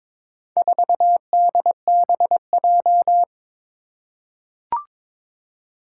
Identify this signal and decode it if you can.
Morse code, heard as a keyed tone, about 22 wpm, 700 Hz. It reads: 4DBJ
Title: Four Challenging Letter-Number Combinations 22wpm